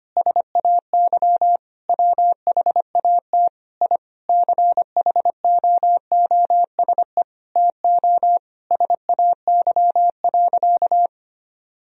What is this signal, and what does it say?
Morse code, heard as a keyed tone, about 25 wpm, 700 Hz. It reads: SAY WHAT I CHOOSE TO SAY.
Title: Sentences from Top 1000 Words - Review Part 4 - 25wpm